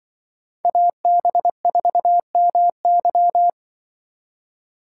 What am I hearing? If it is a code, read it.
Morse code, heard as a keyed tone, about 25 wpm, 700 Hz. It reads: AB4MY
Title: Speed Racing - USA 2x2 Call Signs 20wpm